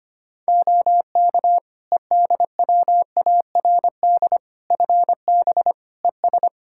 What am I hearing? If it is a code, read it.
Morse code, heard as a keyed tone, about 25 wpm, 700 Hz. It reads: OK EDWARD FB ES
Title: Practice QSO - Exchange 2 - Info - 25wpm